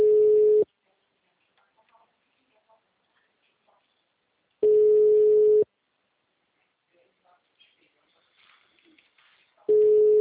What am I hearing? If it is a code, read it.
Ringback tone (Europe, 425 Hz), with speech.